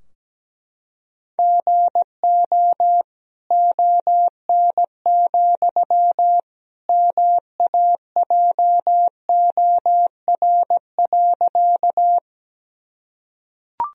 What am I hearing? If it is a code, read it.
Morse code, heard as a keyed tone, about 17 wpm, 700 Hz. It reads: GO ON, MAJOR.
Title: Sentences from Top 1000 Words - Review Part 4 - 17wpm